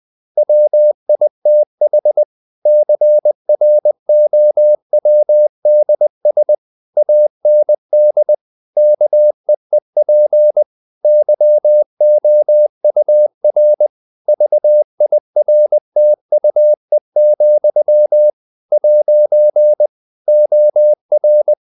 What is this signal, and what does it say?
Morse code, heard as a keyed tone, about 20 wpm, 600 Hz. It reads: WITH CROWDS AND KEEP YOUR VIRTUE, ' OR
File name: CW / Practice / If20